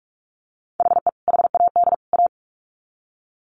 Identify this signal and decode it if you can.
Morse code, heard as a keyed tone, about 44 wpm, 700 Hz. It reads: HE HAD A
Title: Sets of 3 Words 45wpm